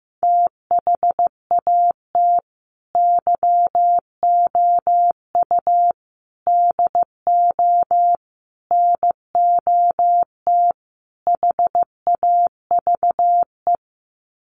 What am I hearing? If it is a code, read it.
Morse code, heard as a keyed tone, about 14 wpm, 700 Hz. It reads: THAT YOU DO NOT HAVE
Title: Sets of 5 Words Encore 15wpm